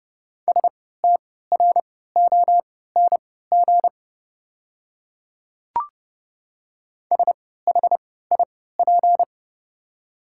Morse code, heard as a keyed tone, about 30 wpm, spaced as for 17 wpm, 690 Hz. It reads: STRONG SHIP
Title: 3x Character Spacing - Top 500 Words - 30wpm